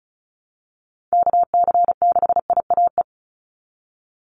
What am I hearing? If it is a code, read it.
Morse code, heard as a keyed tone, about 34 wpm, 700 Hz. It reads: KC6IAE
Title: USA Call Signs 35wpm